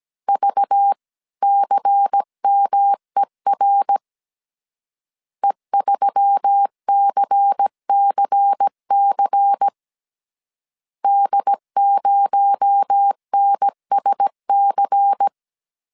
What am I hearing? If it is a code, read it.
Morse code, heard as a keyed tone, about 17 wpm, 800 Hz. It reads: VCMER E3CCC D0NSC
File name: 16wpm